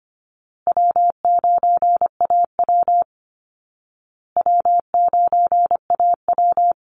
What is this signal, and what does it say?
Morse code, heard as a keyed tone, about 25 wpm, 700 Hz. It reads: W9AW W9AW
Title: Easy Call Signs Repeated 3x - 25wpm